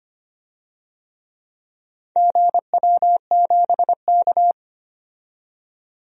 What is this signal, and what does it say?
Morse code, heard as a keyed tone, about 25 wpm, 700 Hz. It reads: GW7K
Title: DXCC-100 Call Signs - 4 Characters - 25wpm